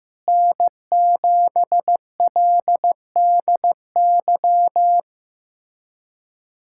Morse code, heard as a keyed tone, about 16 wpm, 700 Hz. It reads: N7LDY
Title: USA Callsign DE Callsign 15wpm